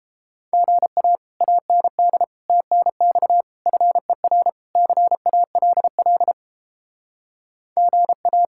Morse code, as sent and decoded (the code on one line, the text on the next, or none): --. .- / .- -. -.. / - -. -..- / ..-. .-. / -.-. .- .-.. .-.. / --. .-
GA AND TNX FER CALL GA